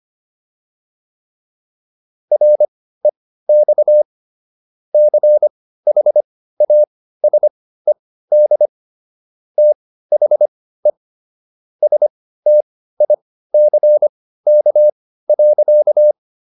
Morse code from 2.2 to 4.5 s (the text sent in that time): REX